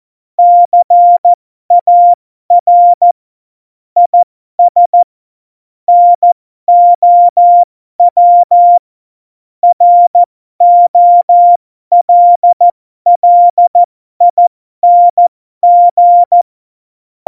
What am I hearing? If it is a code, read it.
Morse code, heard as a keyed tone, about 14 wpm, spaced as for 12 wpm, 700 Hz. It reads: CAR IS NOW ROLLING